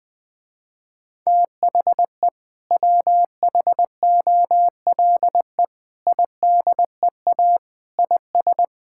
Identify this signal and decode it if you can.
Morse code, heard as a keyed tone, about 20 wpm, 700 Hz. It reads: THE WHOLE IDEA IS
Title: Sentences with 2-Word Phrases - Spoken Beginning - 20wpm